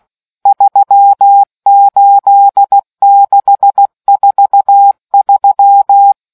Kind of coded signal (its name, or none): Morse code